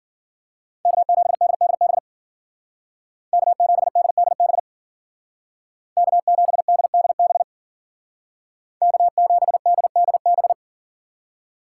Morse code, heard as a keyed tone, about 49 wpm, 700 Hz. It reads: K7DDB K7DDB K7DDB K7DDB
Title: Speed Racing - USA 1x3 Call Signs 40wpm